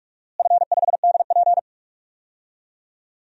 Morse code, heard as a keyed tone, about 41 wpm, 700 Hz. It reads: F5DP
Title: DXCC-100 Call Signs - 4 Characters - Repeated 3x - 45wpm